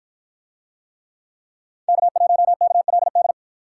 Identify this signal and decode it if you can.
Morse code, heard as a keyed tone, about 48 wpm, 700 Hz. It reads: K1KLD